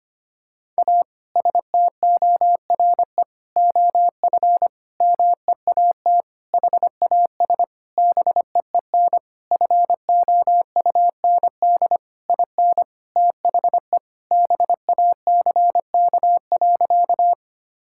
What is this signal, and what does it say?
Morse code, heard as a keyed tone, about 25 wpm, 700 Hz. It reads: A STORE OF MEAT HAS BEEN FOUND IN THE BACK.
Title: Sentences from Top 1000 Words - Review Part 5 - 25wpm